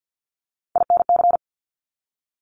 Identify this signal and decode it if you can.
Morse code, heard as a keyed tone, about 50 wpm, 700 Hz. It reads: INC
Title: Top Three Letter Combinations in Words 50wpm